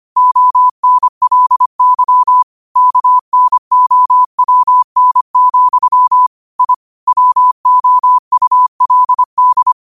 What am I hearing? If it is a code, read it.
Morse code, heard as a keyed tone, about 25 wpm, 1000 Hz. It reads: ONLY KNOWN, I WOULD
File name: qotdfiles / 25wpm